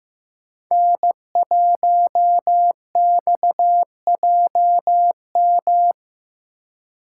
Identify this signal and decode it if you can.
Morse code, heard as a keyed tone, about 15 wpm, 700 Hz. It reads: N1XJM
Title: K1USN SST QSO Practice - 15wpm